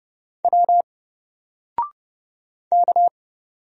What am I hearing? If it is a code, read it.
Morse code, heard as a keyed tone, about 30 wpm, 700 Hz. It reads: W K
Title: Single Character (Letter:Number:Punctuation:Prosign) Alt 30wpm